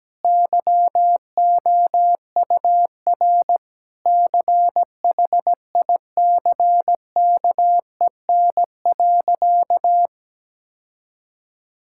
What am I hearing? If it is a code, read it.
Morse code, heard as a keyed tone, about 17 wpm, 700 Hz. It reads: YOUR CHICKEN.
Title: Sentences from Top 1000 Words - Review Part 5 - 17wpm